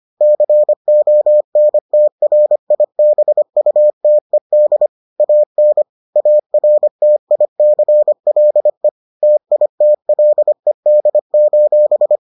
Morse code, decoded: CONTRIBUTED AN ARTICLE TITLED:

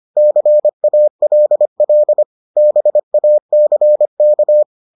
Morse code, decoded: CALL BACK

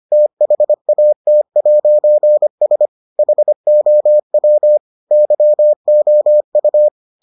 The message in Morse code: THAT'S HOW YOU

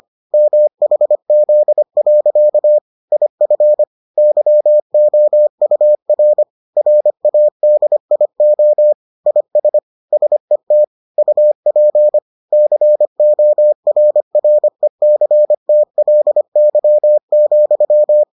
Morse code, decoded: MHZ. IF YOUR RADIO IS SET UP CORRECTLY,

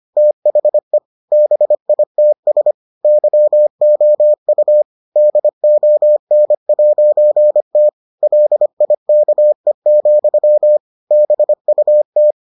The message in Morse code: THE BITS YOU DON'T LIKE, BUT

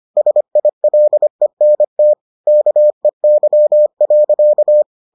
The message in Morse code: SILENT KEY.

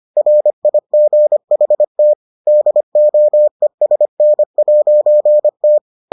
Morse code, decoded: RIGHT DOESN'T

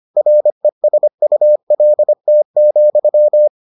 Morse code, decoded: RESULT,

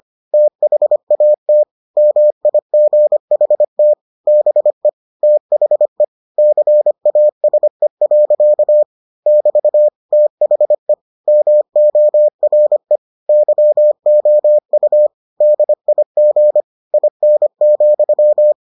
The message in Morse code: THAT MIGHT BE THE CASE. <BT> THE MORE YOU DIG IN,

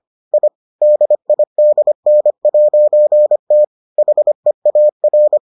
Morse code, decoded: I DIDN'T HEAR

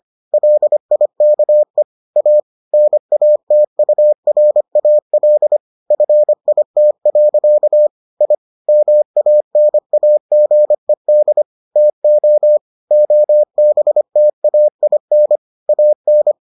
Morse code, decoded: LIKE A NATURAL FIT. I MANAGED TO OBTAIN AN

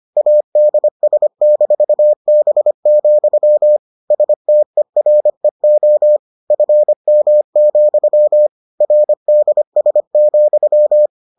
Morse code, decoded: ADS-B, STEREO FM, RDS,

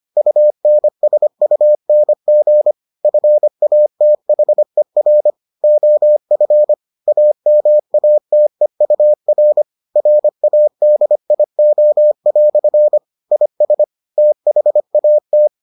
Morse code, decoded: UNSUNG FATHER OF AMATEUR RADIO" IS THAT

